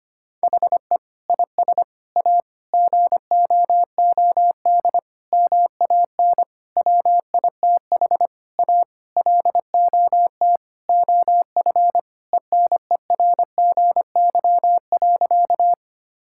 Morse code, heard as a keyed tone, about 25 wpm, 700 Hz. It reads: HE IS A GOOD MAN WITH A LOT OF ENERGY.
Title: Sentences from Top 1000 Words - Review Part 6 - 25wpm